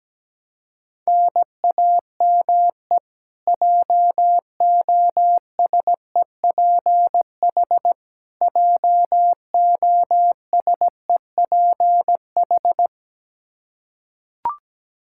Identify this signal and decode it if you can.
Morse code, heard as a keyed tone, about 17 wpm, 700 Hz. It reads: NAME JOSEPH JOSEPH
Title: K1USN SST QSO Practice 17wpm